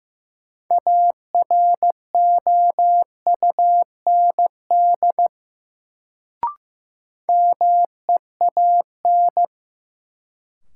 Morse code, heard as a keyed tone, about 15 wpm, 700 Hz. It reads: AROUND MEAN
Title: Sets of 2 Words - New Word Drill - 15wpm